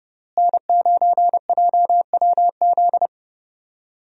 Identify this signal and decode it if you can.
Morse code, heard as a keyed tone, about 30 wpm, 700 Hz. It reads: N9JWZ
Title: USA Call Signs - 5 Characters - Repeated 2x - 30wpm